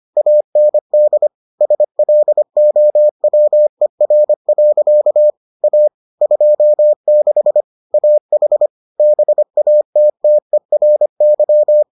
Morse code, heard as a keyed tone, about 25 wpm, 600 Hz. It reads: AND SLOWER. A 26 AH BATTERY